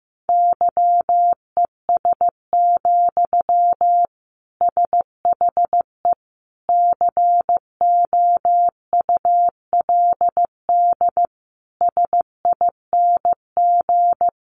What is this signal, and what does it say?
Morse code, heard as a keyed tone, about 15 wpm, 700 Hz. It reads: YES, SHE COULD SING
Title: Sentences from Top 500 Words 15wpm